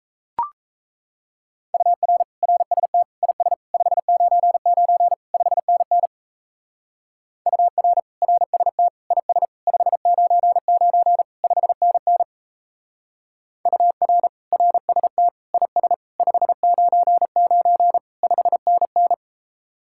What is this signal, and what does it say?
Morse code, heard as a keyed tone, about 37 wpm, 700 Hz. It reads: UR RST IS 599 5NN UR RST IS 599 5NN UR RST IS 599 5NN